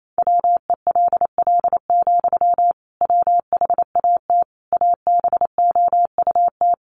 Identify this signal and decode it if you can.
Morse code, heard as a keyed tone, about 28 wpm, 700 Hz. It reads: WELL, WHAT ABOUT
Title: Sentences from Top 1200 Words 28wpm